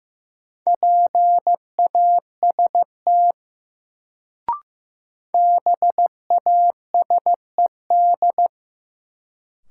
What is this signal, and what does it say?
Morse code, heard as a keyed tone, about 15 wpm, 700 Hz. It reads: PAST BASED
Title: Sets of 2 Words - New Word Drill - 15wpm